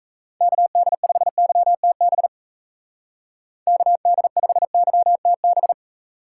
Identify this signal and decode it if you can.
Morse code, heard as a keyed tone, about 33 wpm, 700 Hz. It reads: KD5YTB KD5YTB
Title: Speed Racing - USA 2x3 Call Signs 28wpm